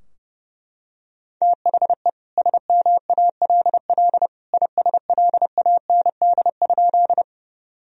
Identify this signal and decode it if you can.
Morse code, heard as a keyed tone, about 30 wpm, 700 Hz. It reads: THE SMALL ISLAND?
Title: Sentences from Top 1000 Words - Review Part 3 - 30wpm